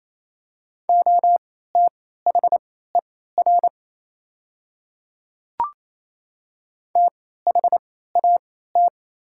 Morse code, heard as a keyed tone, about 28 wpm, spaced as for 16 wpm, 690 Hz. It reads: OTHER THAT
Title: 3x Character Spacing - Top 100 Words - 28wpm